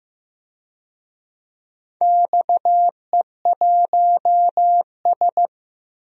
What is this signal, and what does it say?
Morse code, heard as a keyed tone, about 15 wpm, 700 Hz. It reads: XE1S
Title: DXCC-100 Call Signs - 4 Characters - 15wpm